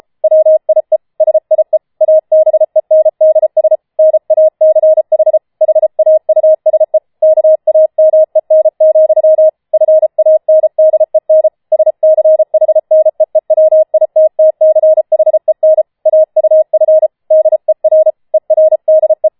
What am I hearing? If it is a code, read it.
Morse code, heard as a keyed tone, about 33 wpm, 630 Hz. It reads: WIE SIE ABENDS NACH HAUSE KAMEN, FANDEN SCHNEEWITTCHEN AUF DER ERDE
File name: Schneewittchen Teil5 34wpm